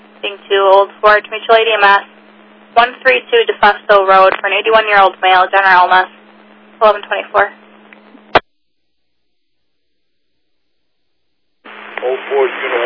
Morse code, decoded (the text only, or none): none